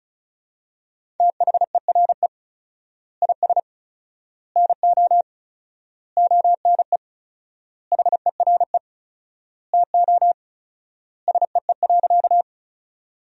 Morse code, heard as a keyed tone, about 34 wpm, 700 Hz. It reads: THERE IS NO ONE HERE TO SEE.
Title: 4x Word-Spacing - Sentences from Top 200 Words - 35wpm